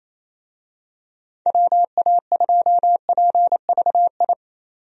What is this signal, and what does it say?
Morse code, heard as a keyed tone, about 28 wpm, 700 Hz. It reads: WA2PVI
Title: USA Call Signs - 6 Characters - 28wpm